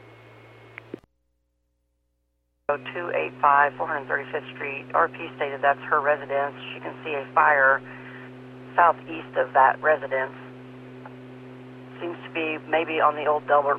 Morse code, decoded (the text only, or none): none